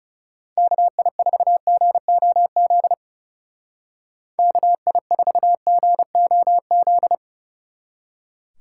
Morse code, heard as a keyed tone, about 37 wpm, 700 Hz. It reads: KI4GOZ KI4GOZ
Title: Speed Racing - USA 2x3 Call Signs 30wpm